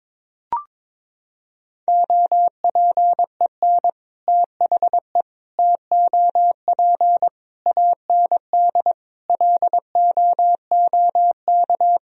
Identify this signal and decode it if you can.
Morse code, heard as a keyed tone, about 21 wpm, 700 Hz. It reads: OPEN THE TOP AND LOOK
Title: Sentences from Top 1000 Words - Review Part 2 - 22wpm